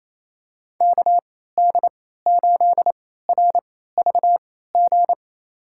Morse code, heard as a keyed tone, about 28 wpm, spaced as for 16 wpm, 700 Hz. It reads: KD8RVG
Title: USA Call Signs - 6 Characters - 3x Character Spacing - 28wpm